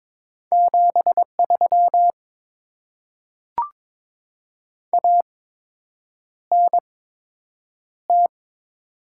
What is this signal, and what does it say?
Morse code, heard as a keyed tone, about 22 wpm, spaced as for 8 wpm, 700 Hz. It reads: 73 ANT
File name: Top 20 QSO Elements - 8x Character Spacing - 22wpm